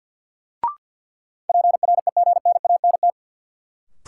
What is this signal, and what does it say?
Morse code, heard as a keyed tone, about 53 wpm, 700 Hz. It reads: PREGNANT
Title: Sentences from Top 1000 Words - Review Part 6 - New Word Drill - 50wpm